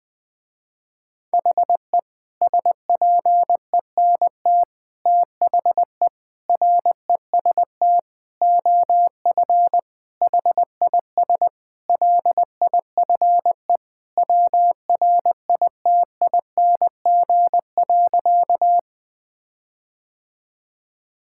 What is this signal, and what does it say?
Morse code, heard as a keyed tone, about 20 wpm, 700 Hz. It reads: HE SPENT THE REST OF HIS LIFE WRITING.
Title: Sentences with 5-Word Phrases - Spoken Beginning - 20wpm